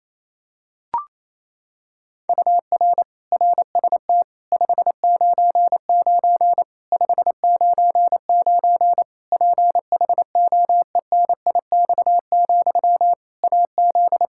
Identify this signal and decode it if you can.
Morse code, heard as a keyed tone, about 28 wpm, 700 Hz. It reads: UR RST 599 599 PHOENIX, AZ
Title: K1USN SST QSO Practice 28wpm